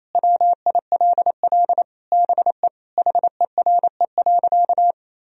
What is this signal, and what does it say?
Morse code, heard as a keyed tone, about 28 wpm, 700 Hz. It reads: WILL BE HERE.
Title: Sentences from Top 1000 Words - Review Part 4 - 28wpm